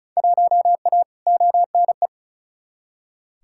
Morse code, heard as a keyed tone, about 36 wpm, 700 Hz. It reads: 1A ONE
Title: Field Day Exchanges Encore - 35wpm